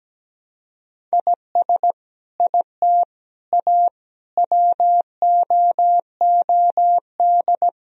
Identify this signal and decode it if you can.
Morse code, heard as a keyed tone, about 17 wpm, 700 Hz. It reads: IS IT A WOOD
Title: Sentences from Top 1000 Words - Review Part 2 - 17wpm